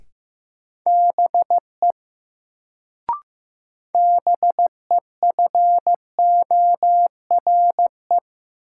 Morse code, heard as a keyed tone, about 15 wpm, 700 Hz. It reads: BE BEFORE
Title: Sets of 2 Words - New Word Drill - 15wpm